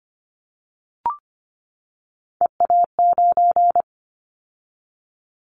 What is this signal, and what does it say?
Morse code, heard as a keyed tone, about 25 wpm, 700 Hz. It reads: EA9
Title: Callsign Prefixes for Top 100 DXCC - 25wpm